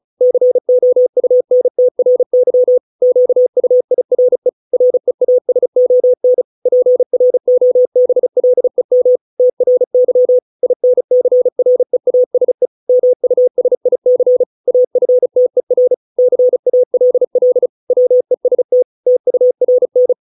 Morse code, heard as a keyed tone, about 36 wpm, 500 Hz. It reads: COUNTRY QUIRE REASON PROBLEM TRY INCREASE MUSIC AFTER CALL WEST TURN